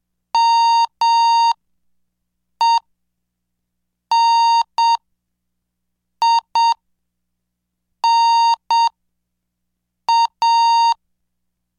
Morse code, heard as a keyed tone, about 7 wpm, spaced as for 5 wpm, 920 Hz. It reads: MENINA